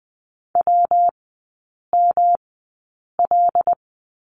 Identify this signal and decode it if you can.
Morse code, heard as a keyed tone, about 20 wpm, spaced as for 8 wpm, 700 Hz. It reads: WML
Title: 114 Lesson 19e - Letter w - Three Characters 20wpm